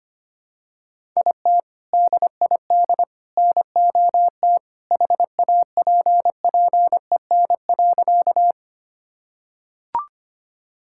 Morse code, heard as a keyed tone, about 25 wpm, 700 Hz. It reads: IT DID NOT HAPPEN.
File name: practice / 25wpm / Sentences with 3-Word Phrases - 25wpm